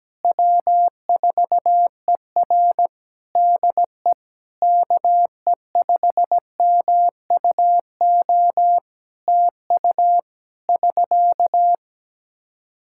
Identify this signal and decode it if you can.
Morse code, heard as a keyed tone, about 17 wpm, 700 Hz. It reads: W4ER DE KE5MUO TU <SK>
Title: K1USN SST QSO Practice 17wpm